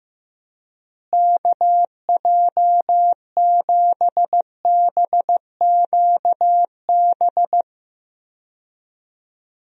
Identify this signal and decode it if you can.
Morse code, heard as a keyed tone, about 15 wpm, 700 Hz. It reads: KJ7BQB
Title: USA Call Signs - 6 Characters - Repeated 3x - 15wpm